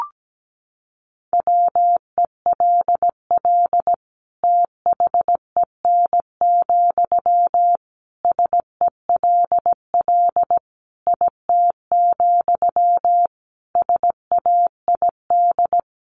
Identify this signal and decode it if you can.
Morse code, heard as a keyed tone, about 17 wpm, 700 Hz. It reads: WELL THEN, SELL IT, SAID
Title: Sentences from Top 1100 Words 17wpm